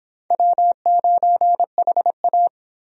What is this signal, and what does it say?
Morse code, heard as a keyed tone, about 31 wpm, 700 Hz. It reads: W9HA